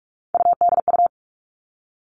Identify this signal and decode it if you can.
Morse code, heard as a keyed tone, about 46 wpm, 700 Hz. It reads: UDU
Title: Three Challenging Letter-Number Combinations 45wpm